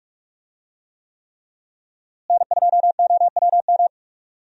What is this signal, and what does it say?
Morse code, heard as a keyed tone, about 44 wpm, 700 Hz. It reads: N2OWM
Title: USA Call Signs - 5 Characters - Repeated 3x - 45wpm